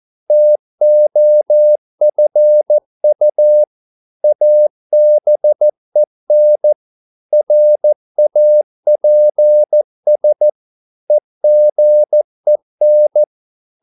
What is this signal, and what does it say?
Morse code, heard as a keyed tone, about 14 wpm, 600 Hz. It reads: TOFU ABEN RAPS EGEN
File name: Korte ord DK 14wpm